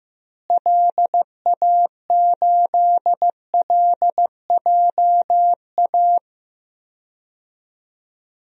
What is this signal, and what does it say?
Morse code, heard as a keyed tone, about 15 wpm, 700 Hz. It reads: LA8LJA
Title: DXCC-100 Call Signs - 6 Characters - Repeated 2x - 15wpm